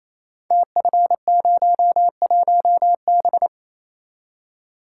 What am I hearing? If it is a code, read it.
Morse code, heard as a keyed tone, about 28 wpm, 700 Hz. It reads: TF01B
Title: Code Groups - 5 Letter:Numbers - 28wpm